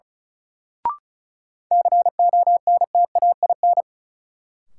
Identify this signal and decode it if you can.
Morse code, heard as a keyed tone, about 34 wpm, 700 Hz. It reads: CONTAIN
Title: Sentences from Top 500 Words - New Word Drill - 35wpm